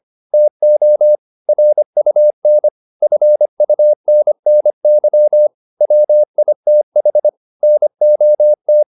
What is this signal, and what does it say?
Morse code, heard as a keyed tone, about 25 wpm, 600 Hz. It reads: TO RUN FUNNY WITH NOT